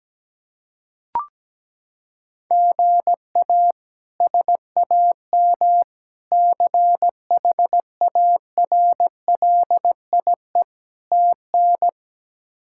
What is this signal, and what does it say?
Morse code, heard as a keyed tone, about 17 wpm, 700 Hz. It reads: GA SAM CHARLIE TN
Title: K1USN SST QSO Practice - 17wpm